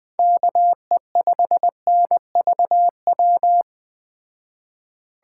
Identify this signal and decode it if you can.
Morse code, heard as a keyed tone, about 20 wpm, 700 Hz. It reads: KE5NVW
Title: Speed Racing - USA 2x3 Call Signs 17wpm